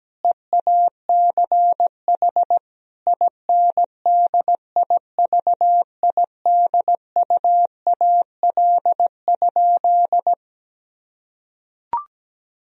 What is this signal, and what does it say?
Morse code, heard as a keyed tone, about 17 wpm, 700 Hz. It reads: EACH INDIVIDUAL?
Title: Sentences from Top 1000 Words - Review Part 4 - 17wpm